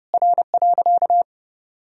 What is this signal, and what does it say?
Morse code, heard as a keyed tone, about 32 wpm, 700 Hz. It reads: R.
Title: Sentences from Top 200 Words and Abbreviations 30wpm